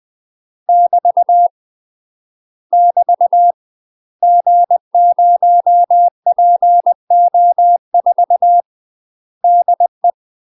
Morse code, heard as a keyed tone, about 20 wpm, 700 Hz. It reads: <BT> <BT> G0PO4 DE
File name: UK MASTER QSOs TRAINING / 20WPM / UK MASTER QSOs 0044